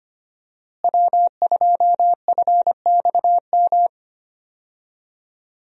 Morse code, heard as a keyed tone, about 25 wpm, 700 Hz. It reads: W2FXM
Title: USA Call Signs - 5 Characters - Repeated 2x - 25wpm